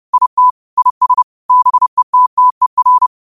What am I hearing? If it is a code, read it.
Morse code, heard as a keyed tone, about 30 wpm, 1000 Hz. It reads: IT IS BETTER
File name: qotdfiles / 30wpm